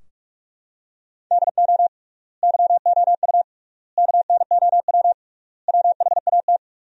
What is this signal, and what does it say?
Morse code, heard as a keyed tone, about 42 wpm, 700 Hz. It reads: DO YOU KNOW WHAT
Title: 3x Word-Spacing - Sentences from Top 100 Words - 45wpm